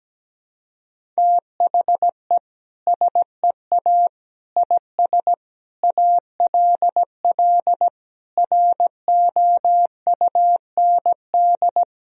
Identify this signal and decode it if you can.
Morse code, heard as a keyed tone, about 17 wpm, 700 Hz. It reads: THE SEA IS ALL ROUND
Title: Sentences from Top 1000 Words - Review Part 1 - 17wpm